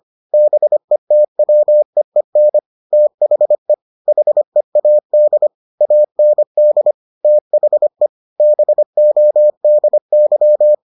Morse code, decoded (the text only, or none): BETWEEN THE HEAD AND THE BODY